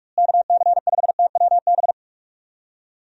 Morse code, heard as a keyed tone, about 43 wpm, 700 Hz. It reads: KC5TWB
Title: USA Call Signs - 6 Characters - Repeated 2x - 45wpm